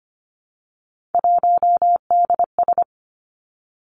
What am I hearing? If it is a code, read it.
Morse code, heard as a keyed tone, about 25 wpm, 700 Hz. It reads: 1DS
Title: Three Challenging Letter-Number Combinations 25wpm